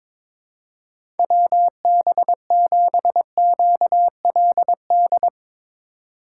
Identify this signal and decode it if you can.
Morse code, heard as a keyed tone, about 22 wpm, 700 Hz. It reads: WB7QLD
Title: USA Call Signs - 6 Characters - 4x Character Spacing - 22wpm